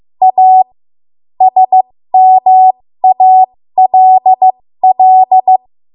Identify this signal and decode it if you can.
Morse code, heard as a keyed tone, about 15 wpm, spaced as for 13 wpm, 750 Hz. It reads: A SMALL
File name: Morse / 13WPM